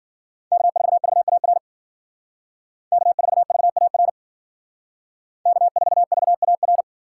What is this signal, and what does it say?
Morse code, heard as a keyed tone, about 52 wpm, 700 Hz. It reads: K4VAR K4VAR K4VAR